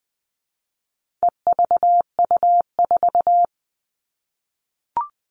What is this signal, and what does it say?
Morse code, heard as a keyed tone, about 20 wpm, 700 Hz. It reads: EVU4
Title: All Cut Numbers - 4-Digit 20wpm